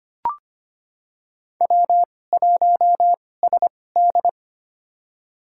Morse code, heard as a keyed tone, about 25 wpm, spaced as for 18 wpm, 700 Hz. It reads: W1SD
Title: USA Call Signs - 4 Characters - 2x Character Spacing - 25wpm